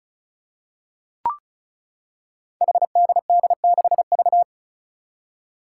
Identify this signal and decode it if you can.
Morse code, heard as a keyed tone, about 34 wpm, 700 Hz. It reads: HDD6V
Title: Five Challenging Letter-Number Combinations 35wpm